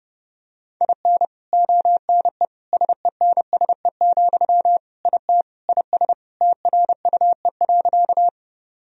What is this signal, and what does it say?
Morse code, heard as a keyed tone, about 28 wpm, 700 Hz. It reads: IN ONE SENSE, IT IS TRUE.
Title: Sentences from Top 1000 Words - Review Part 4 - 30wpm